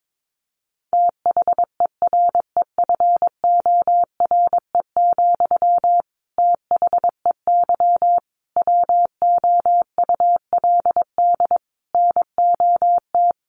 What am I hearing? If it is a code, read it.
Morse code, heard as a keyed tone, about 22 wpm, 700 Hz. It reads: THEREFORE, THEY WOULD NOT
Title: Sentences from Top 800 Words 22wpm